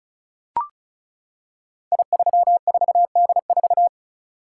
Morse code, heard as a keyed tone, about 36 wpm, 700 Hz. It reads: I34D4
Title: Code Groups - 5 Letter:Numbers - 35wpm